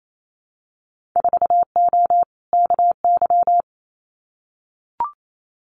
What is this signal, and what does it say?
Morse code, heard as a keyed tone, about 28 wpm, 700 Hz. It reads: 4O KY
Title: Winter Field Day Exchanges Encore 28wpm